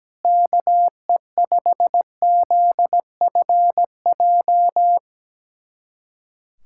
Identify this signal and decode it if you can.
Morse code, heard as a keyed tone, about 17 wpm, 700 Hz. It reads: KE5ZFJ
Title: USA Callsign DE Callsign 17wpm